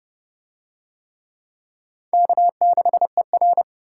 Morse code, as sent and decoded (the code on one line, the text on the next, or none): -.- -.... .-.
K6ER